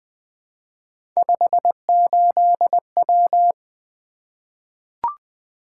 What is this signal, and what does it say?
Morse code, heard as a keyed tone, about 20 wpm, 700 Hz. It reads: 58W
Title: Three Letter-Number Combinations - 20wpm